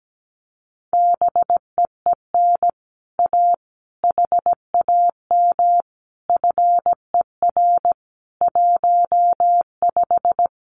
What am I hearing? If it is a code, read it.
Morse code, heard as a keyed tone, about 17 wpm, 700 Hz. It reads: BEEN A HAM FER 15
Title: Common QSO Phrases Encore 17wpm